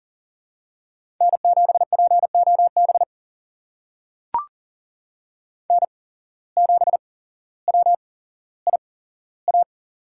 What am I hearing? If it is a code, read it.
Morse code, heard as a keyed tone, about 40 wpm, spaced as for 17 wpm, 700 Hz. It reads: N7POB N7WIA
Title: USA Call Signs - 5 Characters - 8x Character Spacing - 40wpm